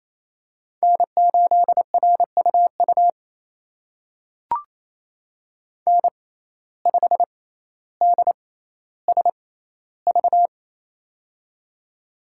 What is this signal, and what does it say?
Morse code, heard as a keyed tone, about 27 wpm, spaced as for 14 wpm, 700 Hz. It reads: N8RUU N5DSV
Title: USA Call Signs - 6x Character Spacing - 28wpm